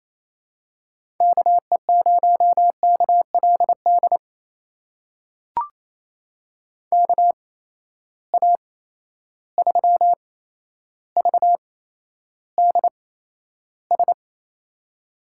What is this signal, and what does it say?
Morse code, heard as a keyed tone, about 28 wpm, spaced as for 12 wpm, 700 Hz. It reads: KE0KLD KA3VDS
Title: USA Call Signs - 6 Characters - 8x Character Spacing - 28wpm